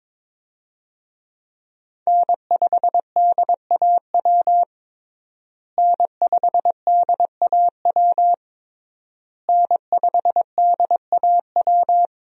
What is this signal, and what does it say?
Morse code, heard as a keyed tone, about 22 wpm, 700 Hz. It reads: N5DAW N5DAW N5DAW
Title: USA Call Signs - 5 Characters - Repeated 3x - 22wpm